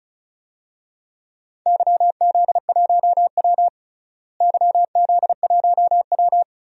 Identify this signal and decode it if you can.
Morse code, heard as a keyed tone, about 34 wpm, 700 Hz. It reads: YZ1W YZ1W
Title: DXCC-100 Call Signs - 4 Characters - Repeated 3x - 35wpm